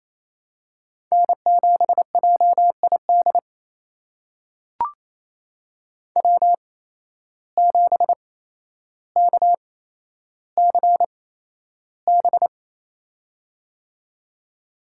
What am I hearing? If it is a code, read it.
Morse code, heard as a keyed tone, about 28 wpm, spaced as for 12 wpm, 700 Hz. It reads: N7JID W7KCB
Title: USA Call Signs - 5 Characters - 8x Character Spacing - 28wpm